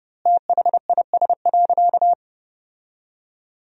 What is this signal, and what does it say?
Morse code, heard as a keyed tone, about 30 wpm, 700 Hz. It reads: THIS.
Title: 4x Word-Spacing - Sentences from Top 400 Words - 30wpm